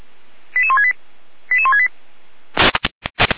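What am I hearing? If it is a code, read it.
Five-tone selective call (ZVEI), heard as quick runs of five tones, tones of about 70 ms. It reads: 88168 88168
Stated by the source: ZVEI five-tone selective call 88447 received on an SDR, recorded by feuerwehr-orlamuende.org (fire brigade Orlamünde).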